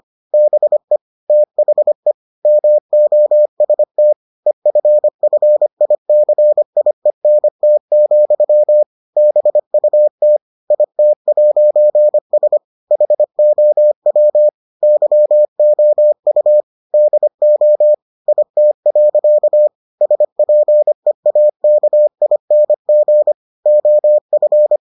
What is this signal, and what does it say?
Morse code, heard as a keyed tone, about 25 wpm, 600 Hz. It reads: BE THE MOST EFFICIENT, BUT IT'S HOW YOU DO IT. SPEAKING OF